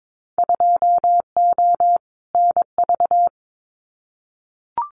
Morse code, heard as a keyed tone, about 22 wpm, 700 Hz. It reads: 2O NV
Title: Winter Field Day Exchanges 22wpm